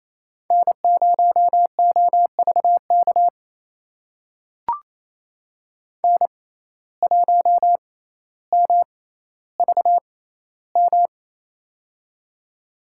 Morse code, heard as a keyed tone, about 28 wpm, spaced as for 14 wpm, 700 Hz. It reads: N0OVK N1MVM
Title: USA Call Signs - 5 Characters - 6x Character Spacing - 28wpm